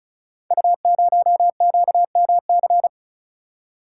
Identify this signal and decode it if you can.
Morse code, heard as a keyed tone, about 36 wpm, 690 Hz. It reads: U0QMC
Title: Code Groups - 5 Letter:Numbers - 35wpm